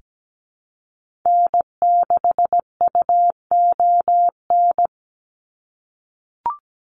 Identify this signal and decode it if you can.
Morse code, heard as a keyed tone, about 17 wpm, 700 Hz. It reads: N6UON
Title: USA Call Signs 17wpm